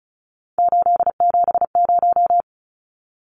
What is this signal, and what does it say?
Morse code, heard as a keyed tone, about 34 wpm, 700 Hz. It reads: 870
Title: Numbers - 3-Digit 35wpm